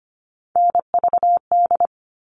Morse code, heard as a keyed tone, about 25 wpm, 700 Hz. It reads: NVD
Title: All Cut Numbers - 3-Digit 25wpm